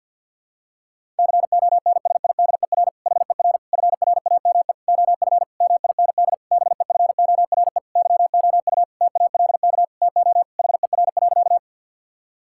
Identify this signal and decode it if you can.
Morse code, heard as a keyed tone, about 51 wpm, 700 Hz. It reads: CONSIDER HER FRAME OF MIND BEFORE YOU TALK TO HER.
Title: Sentences from Top 1000 Words - Review Part 6 - 50wpm